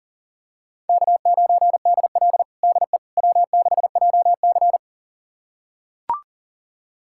Morse code, heard as a keyed tone, about 40 wpm, 700 Hz. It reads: K9DL DE W6JC
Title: USA Callsign DE Callsign - 4 Characters - 40wpm